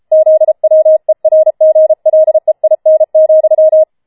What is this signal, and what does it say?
Morse code, heard as a keyed tone, about 33 wpm, 630 Hz. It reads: ZWERGLEIN,
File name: Schneewittchen Teil5 34wpm